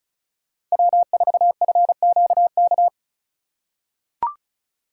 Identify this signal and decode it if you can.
Morse code, heard as a keyed tone, about 36 wpm, 700 Hz. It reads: W4FQK
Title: USA Call Signs - 5 Characters - 2x Character Spacing - 35wpm